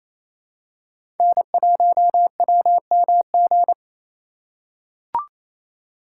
Morse code, heard as a keyed tone, about 28 wpm, 700 Hz. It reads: N1WMG
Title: USA Call Signs - 3x Character Spacing - 28wpm